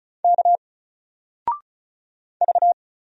Morse code, heard as a keyed tone, about 36 wpm, 700 Hz. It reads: K V
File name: Single-Letter-Number-35wpm